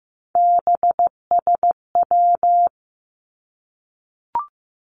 Morse code, heard as a keyed tone, about 15 wpm, 700 Hz. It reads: BSW
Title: Three Challenging Letter-Number Combinations 15wpm